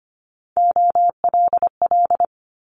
Morse code, heard as a keyed tone, about 25 wpm, 700 Hz. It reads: OLL
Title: Multi-letter Phonemes Quick Fire 25wpm